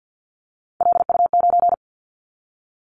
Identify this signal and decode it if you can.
Morse code, heard as a keyed tone, about 50 wpm, 690 Hz. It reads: LU9
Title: Callsign Prefixes for Top 100 DXCC - 50wpm